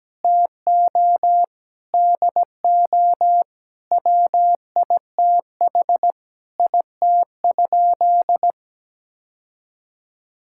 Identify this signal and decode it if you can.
Morse code, heard as a keyed tone, about 17 wpm, 700 Hz. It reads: TO DO WITH IT?
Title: Sentences from Top 1000 Words - Review Part 5 - 17wpm